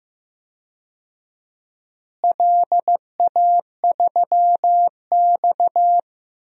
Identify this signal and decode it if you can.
Morse code, heard as a keyed tone, about 15 wpm, 700 Hz. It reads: LA3X
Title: DXCC-100 Call Signs - 4 Characters - Repeated 2x - 15wpm